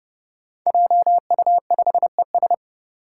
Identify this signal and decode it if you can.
Morse code, heard as a keyed tone, about 28 wpm, 700 Hz. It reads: JU5ES
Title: Five Challenging Letter-Number Combinations 30wpm